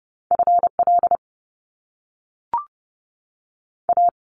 Morse code, heard as a keyed tone, about 30 wpm, 700 Hz. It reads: FL A
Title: Top Two Letter Combinations in Words 30wpm